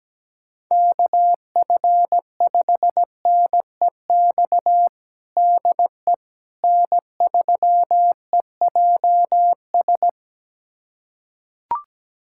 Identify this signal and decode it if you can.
Morse code, heard as a keyed tone, about 17 wpm, 700 Hz. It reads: KF5NEX DE N3EJS
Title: USA Callsign DE Callsign 17wpm